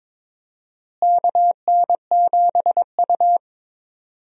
Morse code, heard as a keyed tone, about 22 wpm, 700 Hz. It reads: KN7U
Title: USA Call Signs - 4 Characters - Repeated 3x - 22wpm